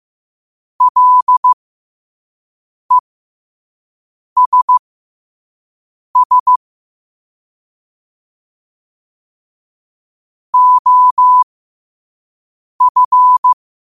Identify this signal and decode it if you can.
Morse code, heard as a keyed tone, about 15 wpm, spaced as for 5 wpm, 1000 Hz. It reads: LESS OF
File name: qotdfiles / 5wpm